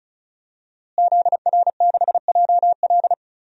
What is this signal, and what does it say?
Morse code, heard as a keyed tone, about 34 wpm, 700 Hz. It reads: ZR6JL
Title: DXCC-100 Call Signs - 5 Characters - Repeated 2x - 35wpm